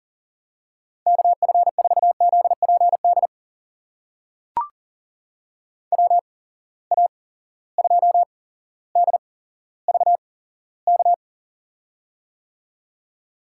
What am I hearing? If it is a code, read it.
Morse code, heard as a keyed tone, about 40 wpm, spaced as for 17 wpm, 700 Hz. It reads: KF4ZPD WA2DVK
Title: USA Call Signs - 6 Characters - 8x Character Spacing - 40wpm